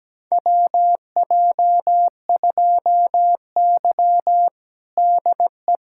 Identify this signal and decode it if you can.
Morse code, heard as a keyed tone, about 17 wpm, 700 Hz. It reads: WJ2Y DE
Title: USA Callsign DE Callsign - 4 Characters - 17wpm